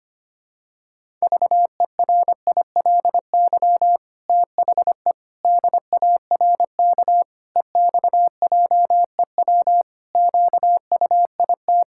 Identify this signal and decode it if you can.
Morse code, heard as a keyed tone, about 25 wpm, 700 Hz. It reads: VERILY THE DARK EXJEW QUIT
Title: Pangram Challenge v2 - 25wpm